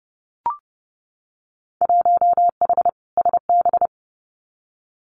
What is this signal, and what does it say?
Morse code, heard as a keyed tone, about 30 wpm, 700 Hz. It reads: 1H SB
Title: Winter Field Day Exchanges 30wpm